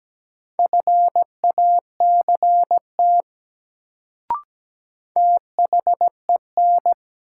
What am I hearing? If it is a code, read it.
Morse code, heard as a keyed tone, about 17 wpm, 700 Hz. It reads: FACT THEN
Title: Sets of 2 Words - New Word Drill - 17wpm